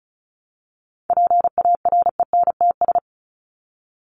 Morse code, heard as a keyed tone, about 34 wpm, 700 Hz. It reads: PARENTS
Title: ES:S Suffix - Top 100 Words with Base Words 35wpm